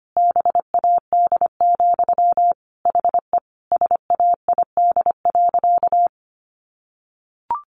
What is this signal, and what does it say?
Morse code, heard as a keyed tone, about 25 wpm, 700 Hz. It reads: BAD, HE SAID.
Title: Sentences from Top 700 Words 25wpm